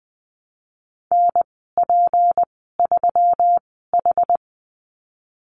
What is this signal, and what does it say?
Morse code, heard as a keyed tone, about 20 wpm, spaced as for 15 wpm, 700 Hz. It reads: NP3H
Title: 167 Lesson 27g - Number 7 - Call Signs 20wpm